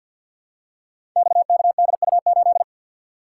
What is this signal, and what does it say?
Morse code, heard as a keyed tone, about 49 wpm, 700 Hz. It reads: XKDU8
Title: Code Groups - 5 Letter:Numbers - 50wpm